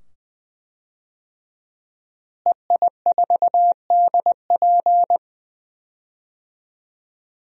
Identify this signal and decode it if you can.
Morse code, heard as a keyed tone, about 20 wpm, 700 Hz. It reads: EI4DP
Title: DXCC-100 Call Signs - 5 Characters - Repeated 2x - 20wpm